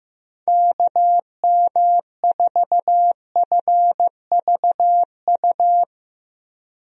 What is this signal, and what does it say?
Morse code, heard as a keyed tone, about 15 wpm, 700 Hz. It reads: KM4FVU
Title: USA Call Signs - 6 Characters - 15wpm